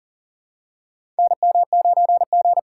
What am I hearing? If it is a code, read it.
Morse code, heard as a keyed tone, about 40 wpm, 700 Hz. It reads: NM9G
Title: USA Call Signs - 4 Characters - 4x Character Spacing - 40wpm